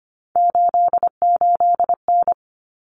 Morse code, heard as a keyed tone, about 25 wpm, 700 Hz. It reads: 88N
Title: Common Cut Numbers - 3-Digit 25wpm